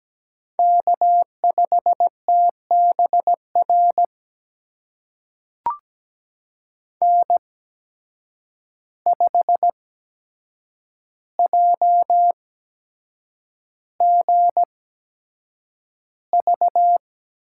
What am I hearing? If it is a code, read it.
Morse code, heard as a keyed tone, about 17 wpm, spaced as for 7 wpm, 700 Hz. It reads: K5TBR N5JGV
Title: USA Call Signs - 8x Character Spacing - 17wpm